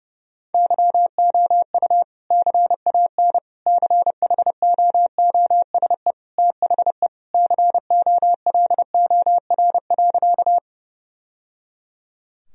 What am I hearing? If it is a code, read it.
Morse code, heard as a keyed tone, about 32 wpm, 700 Hz. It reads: YOU CAN CHOOSE THE COLOR.
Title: Sentences from Top 1000 Words - Review Part 4 - 30wpm